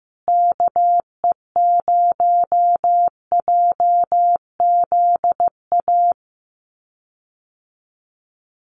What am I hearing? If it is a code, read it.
Morse code, heard as a keyed tone, about 15 wpm, 700 Hz. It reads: KE0JZA
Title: USA Call Signs Repeated 2x 15wpm